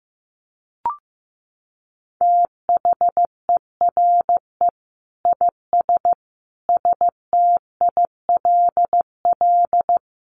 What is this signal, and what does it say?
Morse code, heard as a keyed tone, about 15 wpm, 700 Hz. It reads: THERE IS STILL
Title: Sentences from Top 1000 Words 15wpm